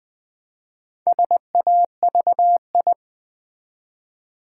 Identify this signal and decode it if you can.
Morse code, heard as a keyed tone, about 20 wpm, 690 Hz. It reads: SAVI
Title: Build A Word - 2 Letters At A Time - 6-Letter Words - 20wpm